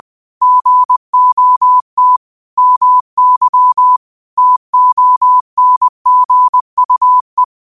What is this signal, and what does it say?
Morse code, heard as a keyed tone, about 20 wpm, 1000 Hz. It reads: GOT MY TONGUE
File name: qotdfiles / 20wpm